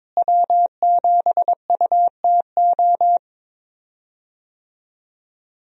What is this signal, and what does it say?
Morse code, heard as a keyed tone, about 22 wpm, 700 Hz. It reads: W7UTO
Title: USA Call Signs - 5 Characters - Repeated 3x - 22wpm